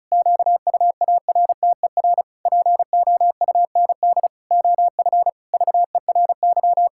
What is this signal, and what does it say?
Morse code, heard as a keyed tone, about 36 wpm, 700 Hz. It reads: QUARTER POUND OF VERY
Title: Pangram Challenge v2 - 35wpm